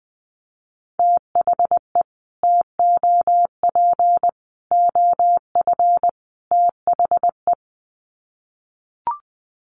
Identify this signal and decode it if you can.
Morse code, heard as a keyed tone, about 19 wpm, 700 Hz. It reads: THE TOP OF THE
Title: Sets of 4 Words 20wpm